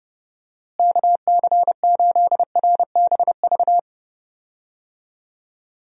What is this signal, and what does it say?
Morse code, heard as a keyed tone, about 30 wpm, 700 Hz. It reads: KC8RBV
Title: USA Call Signs - 6 Characters - Repeated 2x - 30wpm